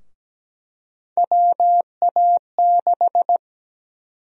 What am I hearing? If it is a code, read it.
Morse code, heard as a keyed tone, about 17 wpm, 700 Hz. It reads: WA6
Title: USA Call Sign Prefixes - 17wpm